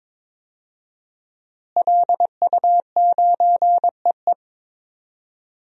Morse code, heard as a keyed tone, about 22 wpm, 700 Hz. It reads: LU9EE
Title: DXCC-100 Call Signs - 5 Characters - Repeated 2x - 22wpm